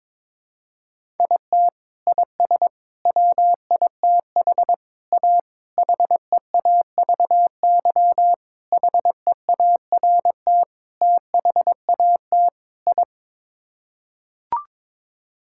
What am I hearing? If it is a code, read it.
Morse code, heard as a keyed tone, about 22 wpm, 700 Hz. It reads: IT IS WITH A HEAVY HEART THAT I
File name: practice / 22wpm / Sets of 8 Words - 22wpm